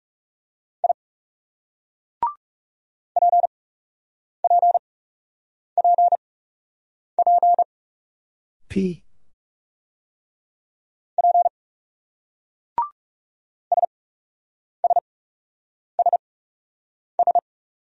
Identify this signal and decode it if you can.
Morse code, heard as a keyed tone, about 38 wpm, spaced as for 8 wpm, 700 Hz. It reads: I PPPP P SSSS